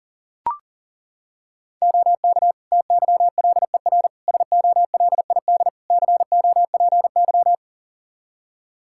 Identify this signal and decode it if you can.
Morse code, heard as a keyed tone, about 40 wpm, 700 Hz. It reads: OK TYLER SOLID COPY
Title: Practice QSO - Exchange 3 - More Info - 40wpm